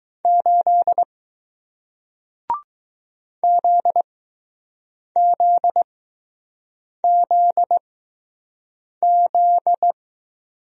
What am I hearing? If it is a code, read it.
Morse code, heard as a keyed tone, about 20 wpm, spaced as for 6 wpm, 700 Hz. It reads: 8 ZZZZ